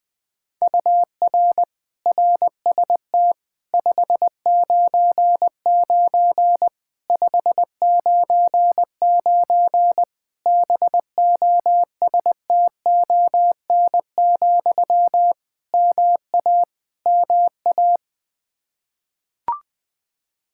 Morse code, heard as a keyed tone, about 20 wpm, 700 Hz. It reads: UR RST 599 599 BOSTON, MA MA
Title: Practice QSO - Exchange 1 - Beginning - 20wpm